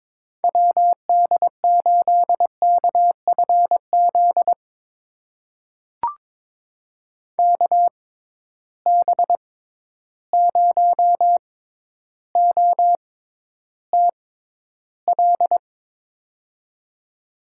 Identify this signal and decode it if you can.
Morse code, heard as a keyed tone, about 21 wpm, spaced as for 11 wpm, 700 Hz. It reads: WD8KFZ KB0OTL
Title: USA Call Signs - 6 Characters - 6x Character Spacing - 22wpm